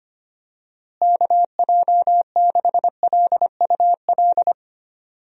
Morse code, heard as a keyed tone, about 25 wpm, 700 Hz. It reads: KJ6LUL
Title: USA Call Signs - 6 Characters - Repeated 2x - 25wpm